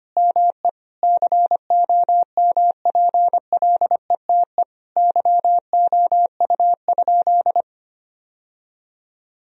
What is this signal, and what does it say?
Morse code, heard as a keyed tone, about 25 wpm, 700 Hz. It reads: ME COMPLETE YOU?
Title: Sentences from Top 1000 Words - Review Part 2 - 25wpm